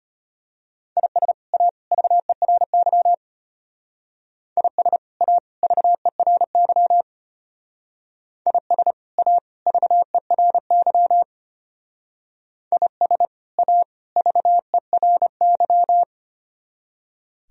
Morse code, heard as a keyed tone, about 30 wpm, 700 Hz. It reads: IS A VERY IS A VERY IS A VERY IS A VERY
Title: Speed-Racing - Sets of 3 Words 25wpm